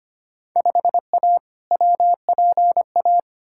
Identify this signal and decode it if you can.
Morse code, heard as a keyed tone, about 25 wpm, 700 Hz. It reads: 5A WPA
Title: Field Day Exchanges Encore - 25wpm